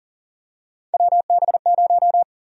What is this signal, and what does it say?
Morse code, heard as a keyed tone, about 40 wpm, 700 Hz. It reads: WB0
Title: USA Call Sign Prefixes - 40wpm